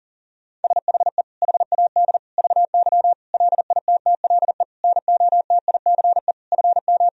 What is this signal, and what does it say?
Morse code, heard as a keyed tone, about 40 wpm, 700 Hz. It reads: SHE HAD VY LITTLE NOTICE FM
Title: Sentences from Top 400 Words and Abbreviations 40wpm